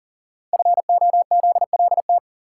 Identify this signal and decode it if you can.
Morse code, heard as a keyed tone, about 42 wpm, 690 Hz. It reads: FOZLT
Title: Code Groups - 5 Letters - 40wpm